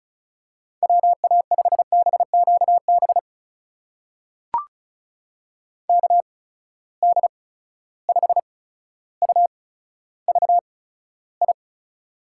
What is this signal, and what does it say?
Morse code, heard as a keyed tone, about 36 wpm, spaced as for 15 wpm, 700 Hz. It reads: WA5BQB KD5UVI
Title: USA Call Signs - 6 Characters - 8x Character Spacing - 35wpm